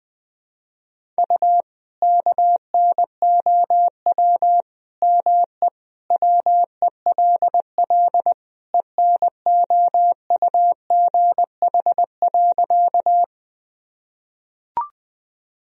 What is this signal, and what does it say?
Morse code, heard as a keyed tone, about 21 wpm, 700 Hz. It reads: U KNOW ME WELL ENOUGH.
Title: Sentences from Top 400 Words and Abbreviations 20wpm